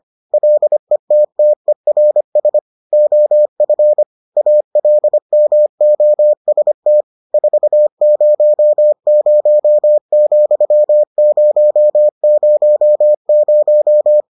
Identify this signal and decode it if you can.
Morse code, heard as a keyed tone, about 25 wpm, 600 Hz. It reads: LETTERS OF ALMOST 400,000